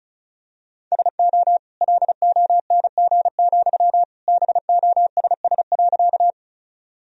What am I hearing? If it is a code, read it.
Morse code, heard as a keyed tone, about 36 wpm, 700 Hz. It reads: SO LONG, BOSS.
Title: Sentences from Top 1000 Words - Review Part 4 - 35wpm